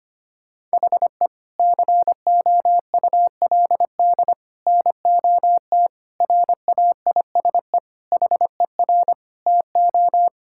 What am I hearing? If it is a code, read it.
Morse code, heard as a keyed tone, about 25 wpm, 700 Hz. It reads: HE COULD NOT RAISE HER TO
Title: Sentences from Top 1600 Words 25wpm